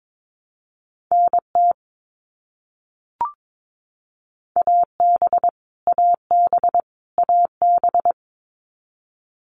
Morse code, heard as a keyed tone, about 22 wpm, 700 Hz. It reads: NT AB AB AB
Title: Canadian Provinces and Territories - Repeated 3x - 22wpm